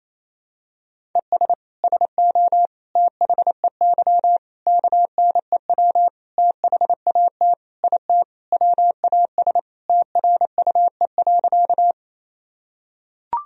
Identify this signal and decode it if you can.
Morse code, heard as a keyed tone, about 27 wpm, 700 Hz. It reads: ES SO THEY KNEW THAT IT WAS TRUE.
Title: Sentences from Top 400 Words and Abbreviations 28wpm